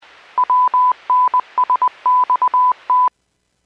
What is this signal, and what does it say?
Morse code, heard as a keyed tone, about 20 wpm, 1000 Hz. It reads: WNSXT